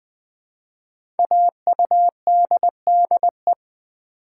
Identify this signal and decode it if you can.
Morse code, heard as a keyed tone, about 20 wpm, 700 Hz. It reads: AUDDE
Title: Code Groups - 5 Letters - 20wpm